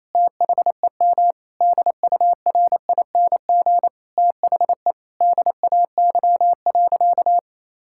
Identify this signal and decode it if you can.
Morse code, heard as a keyed tone, about 28 wpm, 700 Hz. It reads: THEM DURING THE DAY.
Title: Sentences from Top 1000 Words - Review Part 5 - 28wpm